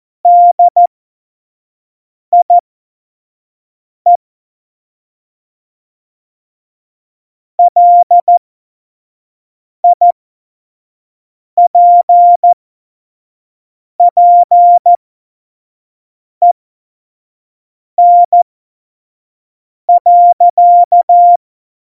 Morse code, heard as a keyed tone, about 14 wpm, spaced as for 5 wpm, 700 Hz. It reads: DIE LIPPEN.
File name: Morsekurs / Witze / Witz 08. Januar 2019, 25 F70